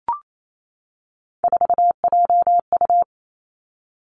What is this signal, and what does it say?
Morse code, heard as a keyed tone, about 28 wpm, 700 Hz. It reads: 4JU
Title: Three Challenging Letter-Number Combinations 28wpm